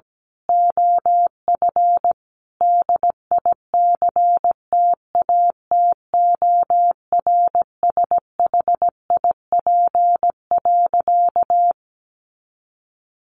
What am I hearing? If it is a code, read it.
Morse code, heard as a keyed tone, about 17 wpm, 700 Hz. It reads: OF DICTATORSHIP.
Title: A Sound Of Thunder 17wpm